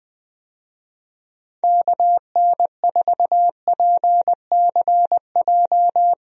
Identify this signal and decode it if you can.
Morse code, heard as a keyed tone, about 20 wpm, 700 Hz. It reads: KN4PCJ
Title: USA Call Signs - 6 Characters - Repeated 3x - 20wpm